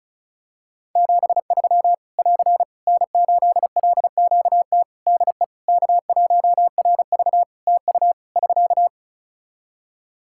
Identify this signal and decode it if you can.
Morse code, heard as a keyed tone, about 36 wpm, 700 Hz. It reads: 73 <AR> N8LQT DE K1RV TU <SK>
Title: K1USN SST QSO Practice 35wpm